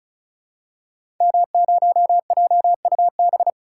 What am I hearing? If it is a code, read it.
Morse code, heard as a keyed tone, about 36 wpm, 700 Hz. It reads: M0JUB
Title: Code Groups - 5 Letter:Numbers - 35wpm